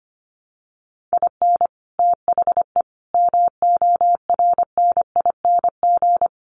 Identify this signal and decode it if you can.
Morse code, heard as a keyed tone, about 25 wpm, 700 Hz. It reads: IN THE MORNING
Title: Sets of 3 Words 25wpm